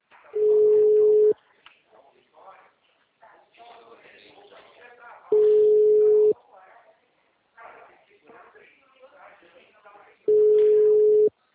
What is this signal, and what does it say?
Ringback tone (Europe, 425 Hz), with speech.